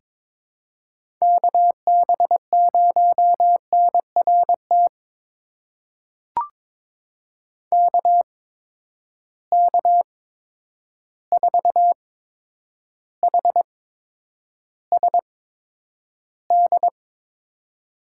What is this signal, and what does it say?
Morse code, heard as a keyed tone, about 22 wpm, spaced as for 9 wpm, 700 Hz. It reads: KB0NRT KK4HSD
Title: USA Call Signs - 6 Characters - 8x Character Spacing - 22wpm